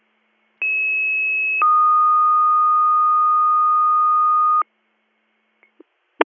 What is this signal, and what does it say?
Two-tone page, heard as a one-second tone, then a three-second tone, about 2600 then 1230 Hz. It is heard from 0.6 to 4.6 s.
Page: bradford-pa